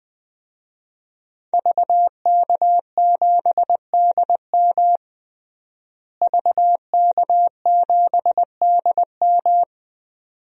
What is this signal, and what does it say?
Morse code, heard as a keyed tone, about 21 wpm, 700 Hz. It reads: VK7DM VK7DM
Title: DXCC-100 Call Signs - 5 Characters - Repeated 2x - 20wpm